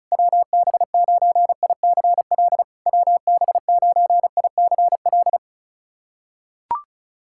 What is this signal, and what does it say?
Morse code, heard as a keyed tone, about 34 wpm, 660 Hz. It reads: WB9ICL WB9ICL
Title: SOTA QSO Practice 35wpm